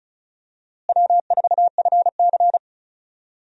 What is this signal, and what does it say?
Morse code, heard as a keyed tone, about 36 wpm, 700 Hz. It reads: W4FC
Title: USA Call Signs - 4 Characters - Repeated 2x - 35wpm